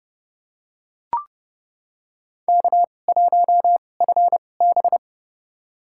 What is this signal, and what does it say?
Morse code, heard as a keyed tone, about 30 wpm, spaced as for 22 wpm, 700 Hz. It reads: K1FB
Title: USA Call Signs - 4 Characters - 2x Character Spacing - 30wpm